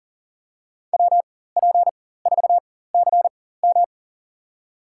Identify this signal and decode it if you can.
Morse code, heard as a keyed tone, about 40 wpm, spaced as for 19 wpm, 700 Hz. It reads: WP4CM
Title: USA Call Signs - 5 Characters - 4x Character Spacing - 40wpm